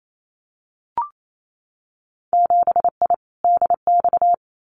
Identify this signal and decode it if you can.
Morse code, heard as a keyed tone, about 28 wpm, 700 Hz. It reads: 7I DX
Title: Winter Field Day Exchanges Encore 28wpm